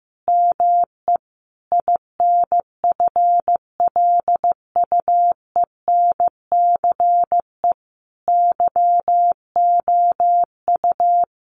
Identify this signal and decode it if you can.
Morse code, heard as a keyed tone, about 15 wpm, 700 Hz. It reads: ME INFLUENCE YOU
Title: Sentences from Top 1100 Words 15wpm